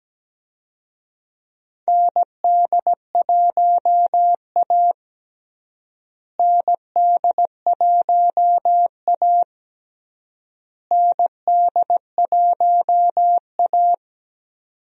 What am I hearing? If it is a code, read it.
Morse code, heard as a keyed tone, about 17 wpm, 700 Hz. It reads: ND1A ND1A ND1A
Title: USA Call Signs - 4 Characters - Repeated 3x - 17wpm